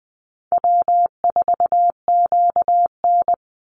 Morse code, heard as a keyed tone, about 20 wpm, 700 Hz. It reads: W4QN
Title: Easy Call Signs Repeated 2x - 20wpm